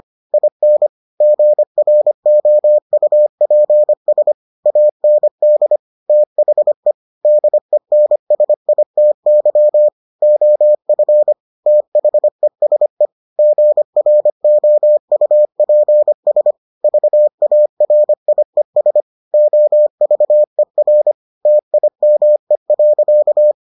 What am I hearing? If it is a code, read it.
Morse code, heard as a keyed tone, about 25 wpm, 600 Hz. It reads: IN GROUPS AND THE DENSITY OF THESE GROUPS VARIES OVER TIME.